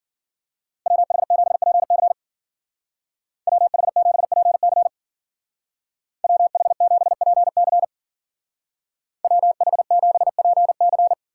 Morse code, heard as a keyed tone, about 50 wpm, 700 Hz. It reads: WH7PC WH7PC WH7PC WH7PC
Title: Speed Racing - USA 2x2 Call Signs 40wpm